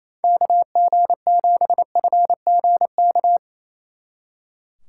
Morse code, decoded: KG7FGK

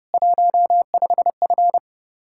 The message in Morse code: 15F

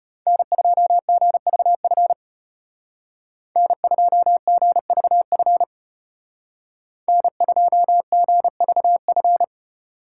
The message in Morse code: N2GVF N2GVF N2GVF

